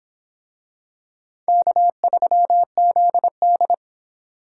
Morse code, heard as a keyed tone, about 24 wpm, 700 Hz. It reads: K3ZD